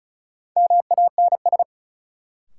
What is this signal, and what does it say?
Morse code, heard as a keyed tone, about 36 wpm, 700 Hz. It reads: MANS
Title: Sentences from Top 1000 Words - Review Part 6 - New Word Drill - 35wpm